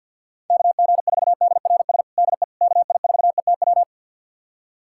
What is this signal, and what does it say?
Morse code, heard as a keyed tone, about 46 wpm, 690 Hz. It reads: KG4DRS DE KI4ETW
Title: USA Callsign DE Callsign - 6 Characters - 50wpm